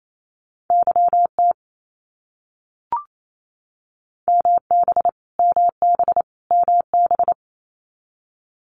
Morse code, heard as a keyed tone, about 28 wpm, 700 Hz. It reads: YT MB MB MB
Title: Canadian Provinces and Territories - Repeated 3x - 28wpm